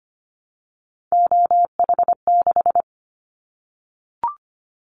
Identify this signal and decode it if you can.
Morse code, heard as a keyed tone, about 25 wpm, 700 Hz. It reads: OH6
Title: Callsign Prefixes for Top 50 DXCC - 25wpm